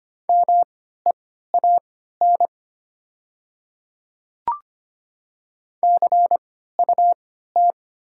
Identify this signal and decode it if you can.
Morse code, heard as a keyed tone, about 25 wpm, spaced as for 14 wpm, 700 Hz. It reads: MEAN CUT
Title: 3x Character Spacing - Top 300 Words - 25wpm